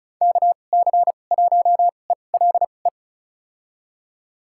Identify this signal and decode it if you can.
Morse code, heard as a keyed tone, about 36 wpm, spaced as for 25 wpm, 700 Hz. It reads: KC1ELE
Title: USA Call Signs - 6 Characters - 2x Character Spacing - 35wpm